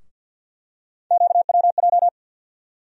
Morse code, heard as a keyed tone, about 52 wpm, 700 Hz. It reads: QWJ
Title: USA Call Sign Postfixes - 3 Letters 50wpm